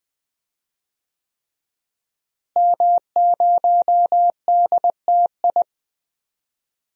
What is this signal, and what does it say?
Morse code, heard as a keyed tone, about 20 wpm, 700 Hz. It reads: M0DTI
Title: DXCC-100 Call Signs - 5 Characters - Repeated 3x - 20wpm